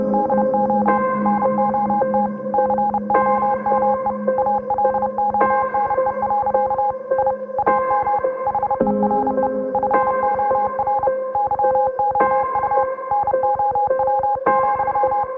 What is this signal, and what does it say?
Morse code, heard as a keyed tone, about 31 wpm, 820 Hz. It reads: DO NOT COME ALONE. I AM HERE FOR YOU NOW,